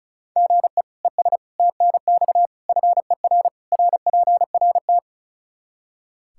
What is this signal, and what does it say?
Morse code, heard as a keyed tone, about 36 wpm, 700 Hz. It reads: GE ES TNX FER RPRT
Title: Practice QSO - Exchange 1 - Beginning - 35wpm